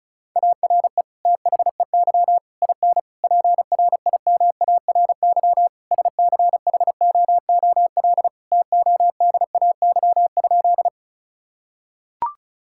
Morse code, heard as a keyed tone, about 36 wpm, 700 Hz. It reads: ARE THEY IN PRIMARY SCHOOL TODAY?
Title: Sentences from Top 1000 Words - Review Part 5 - 35wpm